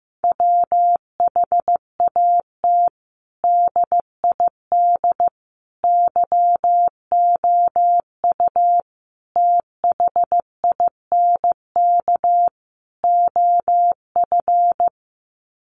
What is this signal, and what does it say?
Morse code, heard as a keyed tone, about 15 wpm, 700 Hz. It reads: WHAT DID YOU THINK OF
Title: Sets of 5 Words Encore2 15wpm